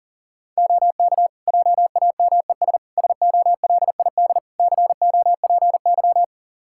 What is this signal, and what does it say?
Morse code, heard as a keyed tone, about 40 wpm, 700 Hz. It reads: OK JAMES SOLID COPY
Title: Practice QSO - Exchange 3 - More Info - 40wpm